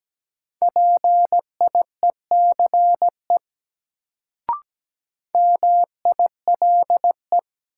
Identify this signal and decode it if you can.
Morse code, heard as a keyed tone, about 17 wpm, 700 Hz. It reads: PIECE MILE
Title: Sentences from Top 400 Words - New Word Drill - 17wpm